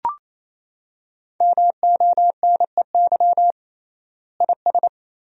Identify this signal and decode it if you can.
Morse code, heard as a keyed tone, about 28 wpm, 700 Hz. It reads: MONEY IS
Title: 3x Word-Spacing - Sentences from Top 500 Words - 28wpm